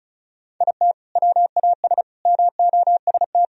Morse code, heard as a keyed tone, about 36 wpm, 700 Hz. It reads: IT WAS MOST
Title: Sentences from Top 1000 Words - Review Part 1 - 35wpm